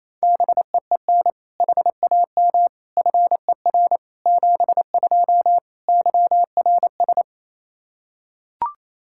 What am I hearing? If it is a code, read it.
Morse code, heard as a keyed tone, about 28 wpm, 700 Hz. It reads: BEEN HAM FER 72 YRS
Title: Practice QSO - Exchange 3 - More Info - 28wpm